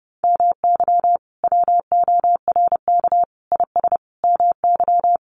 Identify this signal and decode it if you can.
Morse code, heard as a keyed tone, about 30 wpm, 700 Hz. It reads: MY WORK IS MY
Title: Sentences from Top 600 Words 30wpm